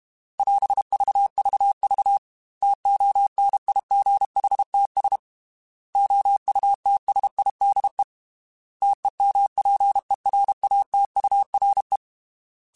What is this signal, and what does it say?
Morse code, heard as a keyed tone, about 32 wpm, 800 Hz. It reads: <AS> VVV TONIGHTS OUTSIDE TEMPERATURE
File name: Weather Sets / Weather Sets-mp3s / Weather 4 17 @30